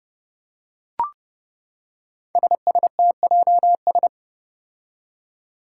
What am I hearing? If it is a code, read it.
Morse code, heard as a keyed tone, about 30 wpm, 700 Hz. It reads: SSTJS
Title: Five Challenging Letter-Number Combinations 30wpm